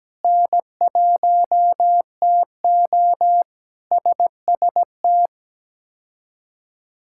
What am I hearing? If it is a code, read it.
Morse code, heard as a keyed tone, about 17 wpm, 700 Hz. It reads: N1TO SST
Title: K1USN SST QSO Practice - 17wpm